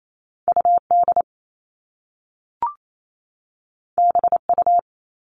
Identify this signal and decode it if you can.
Morse code, heard as a keyed tone, about 28 wpm, 700 Hz. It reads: UD BU
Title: All Cut Numbers - 2-Digit 28wpm